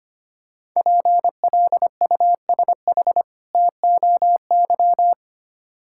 Morse code, heard as a keyed tone, about 25 wpm, 700 Hz. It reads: PLUSH TOY
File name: practice / 25wpm / Difficult Words with Hints - 25wpm